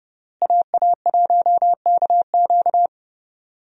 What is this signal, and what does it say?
Morse code, heard as a keyed tone, about 30 wpm, 700 Hz. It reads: AA1KQ
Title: USA Call Signs - 5 Characters - Repeated 3x - 30wpm